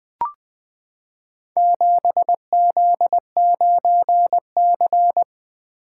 Morse code, heard as a keyed tone, about 20 wpm, 700 Hz. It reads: 7Z9C
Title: Four Letter-Number Combinations - 20wpm